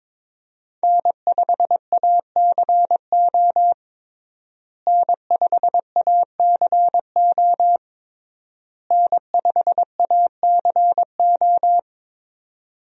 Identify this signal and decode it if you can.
Morse code, heard as a keyed tone, about 23 wpm, 700 Hz. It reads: N5ACO N5ACO N5ACO
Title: USA Call Signs - 5 Characters - Repeated 3x - 22wpm